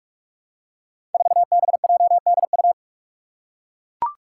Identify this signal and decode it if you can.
Morse code, heard as a keyed tone, about 44 wpm, 700 Hz. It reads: 4BJDU
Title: Five Challenging Letter-Number Combinations 45wpm